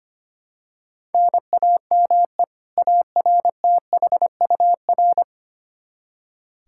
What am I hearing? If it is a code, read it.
Morse code, heard as a keyed tone, about 25 wpm, 700 Hz. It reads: NAME ARTHUR
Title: Practice QSO - Exchange 1 - Beginning - 25wpm